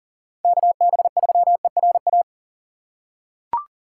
Morse code, heard as a keyed tone, about 40 wpm, 690 Hz. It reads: KB3ERA
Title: USA Call Signs - 6x Character Spacing - 40wpm